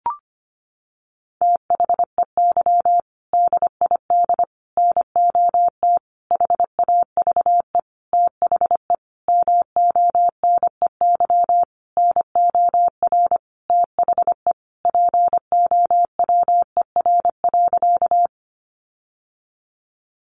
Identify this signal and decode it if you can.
Morse code, heard as a keyed tone, about 25 wpm, 700 Hz. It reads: THEY DID NOT HAVE THE MONEY NOR THE POWER.
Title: Sentences from Top 1500 Words 25wpm